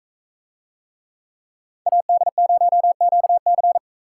Morse code, heard as a keyed tone, about 41 wpm, 700 Hz. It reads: AD0QC